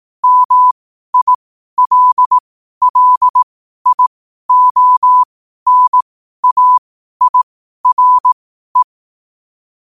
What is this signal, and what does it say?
Morse code, heard as a keyed tone, about 18 wpm, spaced as for 12 wpm, 1000 Hz. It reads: MILLIONAIRE
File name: qotdfiles / 12wpm